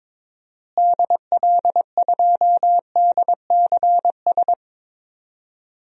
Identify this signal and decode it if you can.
Morse code, heard as a keyed tone, about 22 wpm, 700 Hz. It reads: DL2DCS
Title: DXCC-100 Call Signs - 6 Characters - Repeated 3x - 22wpm